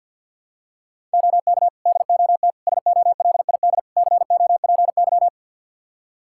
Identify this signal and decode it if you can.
Morse code, heard as a keyed tone, about 53 wpm, 700 Hz. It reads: OK DOT SOLID COPY
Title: Practice QSO - Exchange 3 - More Info - 50wpm